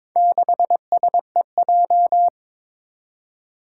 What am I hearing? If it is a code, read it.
Morse code, heard as a keyed tone, about 22 wpm, 700 Hz. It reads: 6SEJ
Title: Four Challenging Letter-Number Combinations 22wpm